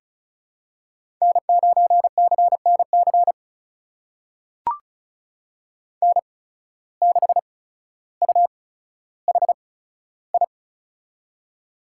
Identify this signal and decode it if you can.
Morse code, heard as a keyed tone, about 36 wpm, spaced as for 15 wpm, 700 Hz. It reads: N9CNC N6UHI
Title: USA Call Signs - 5 Characters - 8x Character Spacing - 35wpm